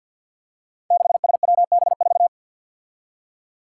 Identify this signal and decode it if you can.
Morse code, heard as a keyed tone, about 46 wpm, 700 Hz. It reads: 6SWB4
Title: Five Challenging Letter-Number Combinations 50wpm